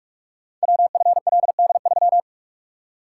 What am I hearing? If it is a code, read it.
Morse code, heard as a keyed tone, about 44 wpm, 700 Hz. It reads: WFLD3
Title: Code Groups - 5 Letter:Numbers - 45wpm